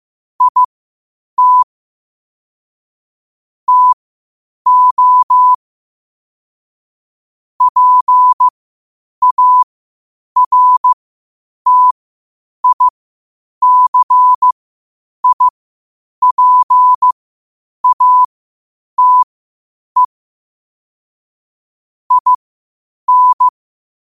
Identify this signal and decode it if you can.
Morse code, heard as a keyed tone, about 15 wpm, spaced as for 8 wpm, 1000 Hz. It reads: IT TO PARTICIPATE IN
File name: qotdfiles / 8wpm